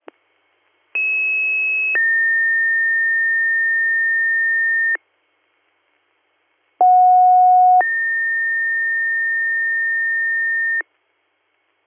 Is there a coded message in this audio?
Yes: two-tone page from 0.9 to 10.8 s, heard as a one-second tone, then a three-second tone, about 2600 then 1800 Hz, about 730 then 1800 Hz.